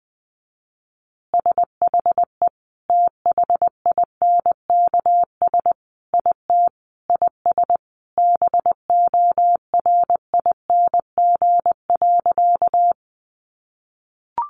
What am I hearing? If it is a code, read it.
Morse code, heard as a keyed tone, about 20 wpm, 700 Hz. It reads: SHE THINKS IT IS BORING.
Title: Easy Phrases 20wpm